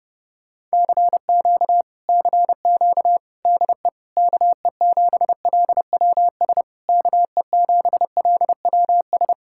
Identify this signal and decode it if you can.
Morse code, heard as a keyed tone, about 33 wpm, 700 Hz. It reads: CQ CQ DE KE7LWS KE7LWS